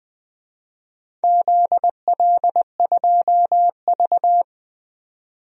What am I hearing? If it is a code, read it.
Morse code, heard as a keyed tone, about 20 wpm, 700 Hz. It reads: ZL2V
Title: DXCC-100 Call Signs - 4 Characters - 20wpm